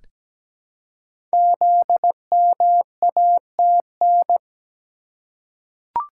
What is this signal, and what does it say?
Morse code, heard as a keyed tone, about 17 wpm, 700 Hz. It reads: ZMATN
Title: Code Groups - 5 Letters - 17wpm